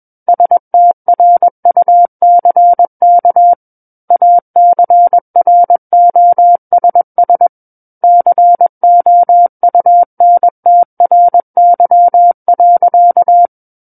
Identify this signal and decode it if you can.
Morse code, heard as a keyed tone, about 21 wpm, 700 Hz. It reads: STRUCK ACROSS COUNTRY.
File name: Book One / 12-Chapter 12 - 21 WPM